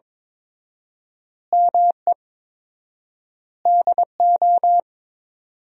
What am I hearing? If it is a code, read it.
Morse code, heard as a keyed tone, about 21 wpm, 700 Hz. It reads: ME DO
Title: 4x Word-Spacing - Sentences from Top 300 Words - 22wpm